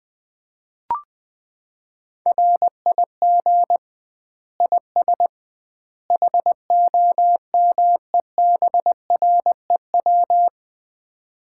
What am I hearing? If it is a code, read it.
Morse code, heard as a keyed tone, about 20 wpm, 700 Hz. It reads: RIG IS HOMEBREW
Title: Common QSO Phrases - 2x Word-Spacing - 20wpm